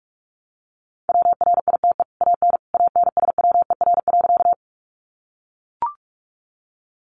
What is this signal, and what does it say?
Morse code, heard as a keyed tone, about 44 wpm, 700 Hz. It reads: WRITE AN ANSWER.
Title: Sentences from Top 300 Words 45wpm